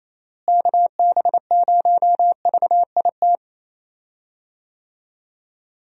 Morse code, heard as a keyed tone, about 28 wpm, 700 Hz. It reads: KB0VIT
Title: USA Call Signs - 6 Characters - Repeated 3x - 28wpm